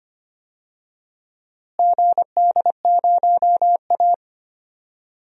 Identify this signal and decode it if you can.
Morse code, heard as a keyed tone, about 25 wpm, 700 Hz. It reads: GD0A
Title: DXCC-100 Call Signs - 4 Characters - Repeated 3x - 25wpm